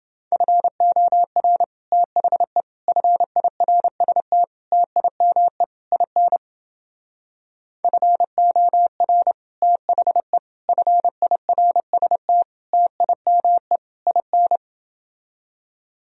Morse code, heard as a keyed tone, about 29 wpm, 700 Hz. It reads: FOR THE FIRST TIME IN FOR THE FIRST TIME IN